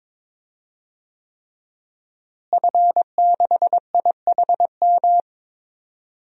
Morse code, heard as a keyed tone, about 22 wpm, 700 Hz. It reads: F6IHM
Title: DXCC-100 Call Signs - 5 Characters - Repeated 3x - 22wpm